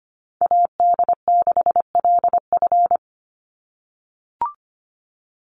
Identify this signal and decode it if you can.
Morse code, heard as a keyed tone, about 25 wpm, 700 Hz. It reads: AD6LF
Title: USA Call Signs 25wpm